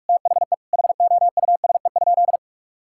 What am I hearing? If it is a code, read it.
Morse code, heard as a keyed tone, about 46 wpm, 700 Hz. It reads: THE HOUSE?
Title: Sentences from Top 1000 Words - Review Part 6 - 45wpm